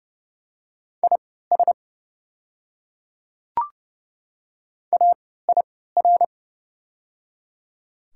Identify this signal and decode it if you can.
Morse code, heard as a keyed tone, about 29 wpm, spaced as for 17 wpm, 700 Hz. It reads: IS AIR
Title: 3x Character Spacing - Top 200 Words - 30wpm